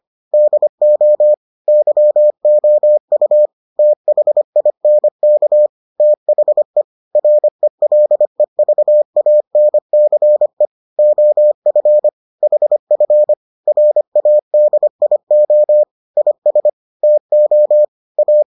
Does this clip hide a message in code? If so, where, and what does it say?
Yes: Morse code from 0.3 to 18.4 s, heard as a keyed tone, about 25 wpm, 600 Hz. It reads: DO YOU THINK THE RELEVANCE OF HF RADIO IS TO A